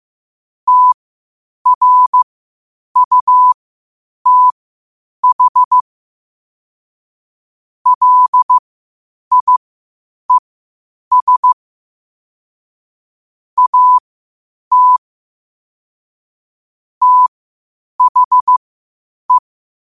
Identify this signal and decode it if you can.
Morse code, heard as a keyed tone, about 15 wpm, spaced as for 8 wpm, 1000 Hz. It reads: TRUTH LIES AT THE
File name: qotdfiles / 8wpm